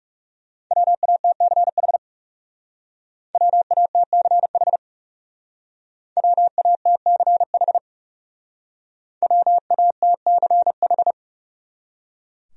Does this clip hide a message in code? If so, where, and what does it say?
Yes: Morse code from 0.7 to 11.1 s, heard as a keyed tone, about 39 wpm, 700 Hz. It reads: WATCH WATCH WATCH WATCH